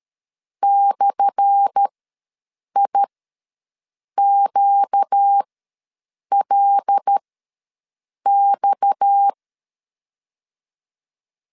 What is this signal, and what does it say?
Morse code, heard as a keyed tone, about 13 wpm, spaced as for 6 wpm, 800 Hz. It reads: /IQLX
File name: LEZIONI / 10 SIGNS / 10A SLASH ws 12wpm 3443